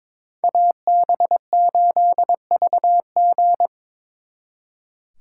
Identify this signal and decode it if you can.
Morse code, heard as a keyed tone, about 26 wpm, 700 Hz. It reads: AB8VG
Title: Speed Racing - USA 2x2 Call Signs 22wpm